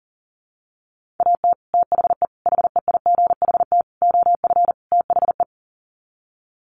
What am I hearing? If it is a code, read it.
Morse code, heard as a keyed tone, about 40 wpm, 700 Hz. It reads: AT THE HEIGHT OF THE
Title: Sets of 5 Words 40wpm